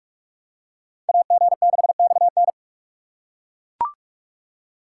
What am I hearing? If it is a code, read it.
Morse code, heard as a keyed tone, about 39 wpm, 700 Hz. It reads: AG6XN